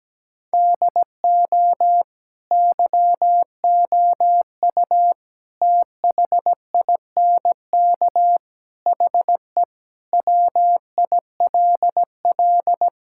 Morse code, heard as a keyed tone, about 17 wpm, 700 Hz. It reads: DO YOU THINK HE WILL
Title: Sentences from Top 1600 Words 17wpm